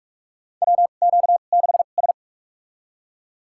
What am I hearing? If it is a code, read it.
Morse code, heard as a keyed tone, about 46 wpm, spaced as for 32 wpm, 700 Hz. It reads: WQ6S
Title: USA Call Signs - 2x Character Spacing - 45wpm